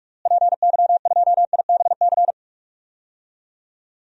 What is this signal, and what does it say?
Morse code, heard as a keyed tone, about 43 wpm, 700 Hz. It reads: PY2IBC
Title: DXCC-100 Call Signs - 6 Characters - Repeated 3x - 45wpm